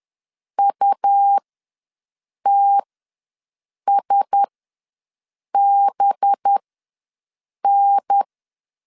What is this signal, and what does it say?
Morse code, heard as a keyed tone, about 10 wpm, spaced as for 6 wpm, 800 Hz. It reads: UTSBN
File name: LEZIONI / 6 PKX0 / 6 PKX0 ws 10wpm 4220